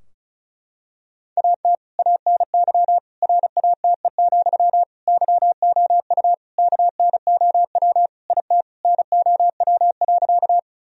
Morse code, heard as a keyed tone, about 36 wpm, 700 Hz. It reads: AT ANY RATE, YOU KNOW IT NOW.
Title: Sentences from Top 1000 Words - Review Part 4 - 35wpm